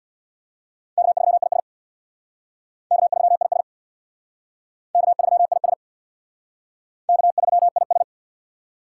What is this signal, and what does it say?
Morse code, heard as a keyed tone, about 59 wpm, 700 Hz. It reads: K3IS K3IS K3IS K3IS
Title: Speed Racing - USA 1x2 Call Signs 50wpm